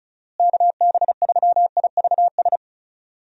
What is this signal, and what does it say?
Morse code, heard as a keyed tone, about 36 wpm, 700 Hz. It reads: KB3IVS
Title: USA Call Signs - 6 Characters - Repeated 2x - 35wpm